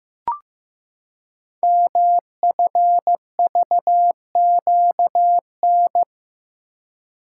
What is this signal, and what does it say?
Morse code, heard as a keyed tone, about 15 wpm, 700 Hz. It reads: MFVQN
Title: Code Groups - 5 Letters - 15wpm